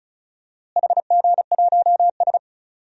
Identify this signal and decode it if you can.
Morse code, heard as a keyed tone, about 34 wpm, 700 Hz. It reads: HG1S
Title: DXCC-100 Call Signs - 4 Characters - Repeated 3x - 35wpm